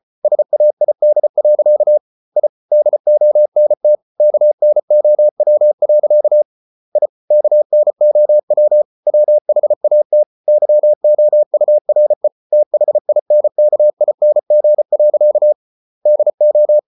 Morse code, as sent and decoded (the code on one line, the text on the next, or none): ... .- .. -.. .-.-.- / .. / -.. --- -. - / -.- -. --- .-- .-.-.- / .. / -.- -. --- .-- / .-- .... .- - / -.-- --- ..- .-. / - .... .. -. -.- .. -. --. .-.-.- / -.. ---
SAID. I DONT KNOW. I KNOW WHAT YOURE THINKING. DO